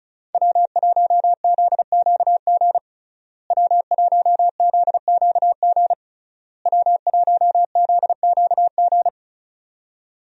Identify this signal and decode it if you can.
Morse code, heard as a keyed tone, about 34 wpm, 700 Hz. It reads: W1ZQG W1ZQG W1ZQG
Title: USA Call Signs - 5 Characters - Repeated 3x - 35wpm